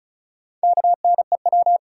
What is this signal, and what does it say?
Morse code, heard as a keyed tone, about 36 wpm, 700 Hz. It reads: KNEW
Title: Sentences from Top 400 Words - New Word Drill - 35wpm